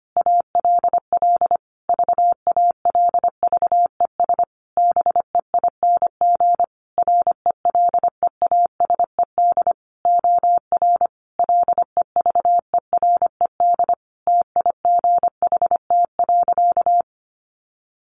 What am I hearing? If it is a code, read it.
Morse code, heard as a keyed tone, about 25 wpm, 700 Hz. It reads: ALL VALVES BEING RELEASED OR LEVERED TIGHT.
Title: A Sound Of Thunder 25wpm